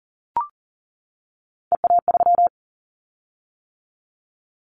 Morse code, heard as a keyed tone, about 41 wpm, 700 Hz. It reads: EA3
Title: Callsign Prefixes for Top 25 DXCC - 40wpm